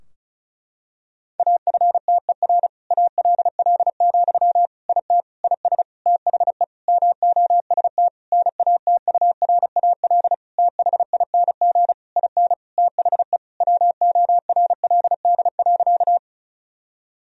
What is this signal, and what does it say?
Morse code, heard as a keyed tone, about 36 wpm, 700 Hz. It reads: AFTER ALL, IT IS THE MOST NATURAL THING IN THE WORLD.
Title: Sentences from Top 1000 Words - Review Part 4 - 35wpm